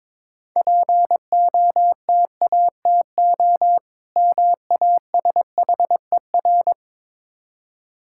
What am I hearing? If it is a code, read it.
Morse code, heard as a keyed tone, about 23 wpm, 690 Hz. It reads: POTATO MASHER
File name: Easy Words - Speed-Racing with Hints - 22wpm